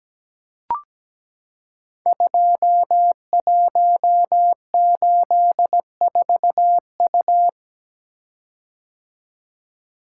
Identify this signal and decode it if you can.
Morse code, heard as a keyed tone, about 17 wpm, 700 Hz. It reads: 2184U
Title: Code Groups - 5 Letter:Numbers - 17wpm